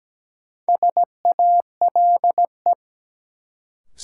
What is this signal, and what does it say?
Morse code, heard as a keyed tone, about 17 wpm, 700 Hz. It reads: SALE
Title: Sentences from Top 1400 Words - New Word Drill - 17wpm